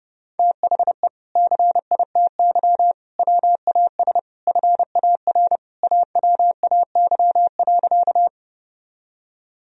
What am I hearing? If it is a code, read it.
Morse code, heard as a keyed tone, about 32 wpm, 700 Hz. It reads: THE CITY WAS FAR AWAY.
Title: Sentences from Top 1000 Words - Review Part 1 - 30wpm